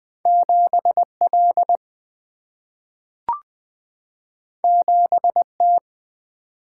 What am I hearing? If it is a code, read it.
Morse code, heard as a keyed tone, about 20 wpm, 700 Hz. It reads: 7L 7T
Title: Single-Letter Single-Number 20wpm